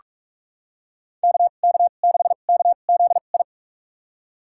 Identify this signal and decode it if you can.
Morse code, heard as a keyed tone, about 46 wpm, spaced as for 32 wpm, 700 Hz. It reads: KK6KZI
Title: USA Call Signs - 6 Characters - 2x Character Spacing - 45wpm